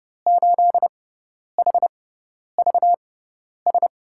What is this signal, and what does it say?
Morse code, heard as a keyed tone, about 30 wpm, spaced as for 10 wpm, 700 Hz. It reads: 8HVS
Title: USA Call Signs - 5 Characters - 6x Character Spacing - 30wpm